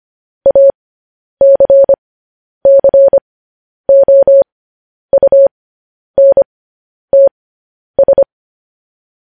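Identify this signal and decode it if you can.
Morse code, heard as a keyed tone, about 25 wpm, spaced as for 10 wpm, 550 Hz. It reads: ACCOUNTS
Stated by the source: Morse code transcription of news headlines sent at 10 wpm.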